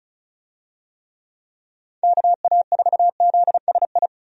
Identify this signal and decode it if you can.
Morse code, heard as a keyed tone, about 36 wpm, 700 Hz. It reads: KA4ZSI
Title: USA Call Signs - 6 Characters - 35wpm